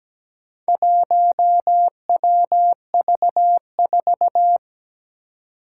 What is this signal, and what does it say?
Morse code, heard as a keyed tone, about 17 wpm, 700 Hz. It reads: 1WV4
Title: Four Challenging Letter-Number Combinations 17wpm